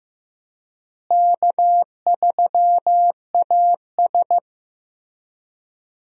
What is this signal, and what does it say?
Morse code, heard as a keyed tone, about 15 wpm, 700 Hz. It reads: K3AS
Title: USA Call Signs - 4 Characters - Repeated 2x - 15wpm